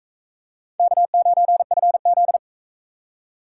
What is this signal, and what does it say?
Morse code, heard as a keyed tone, about 40 wpm, 700 Hz. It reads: K9FZ